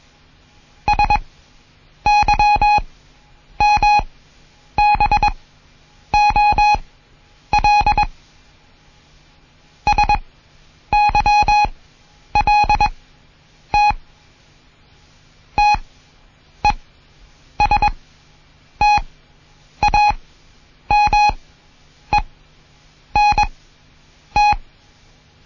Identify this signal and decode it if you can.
Morse code, heard as a keyed tone, about 21 wpm, spaced as for 8 wpm, 830 Hz. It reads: SYMBOL SYLT TESTAMENT
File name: Morsen / Kurs A 20